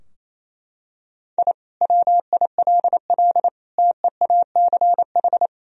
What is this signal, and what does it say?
Morse code, heard as a keyed tone, about 27 wpm, 700 Hz. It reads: I WILL TEACH
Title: Sentences from Top 1000 Words - Review Part 3 - 28wpm